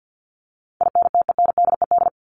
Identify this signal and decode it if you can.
Morse code, heard as a keyed tone, about 50 wpm, 700 Hz. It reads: INTENDED
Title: ED Suffix - Top 100 Words with Base Words 50wpm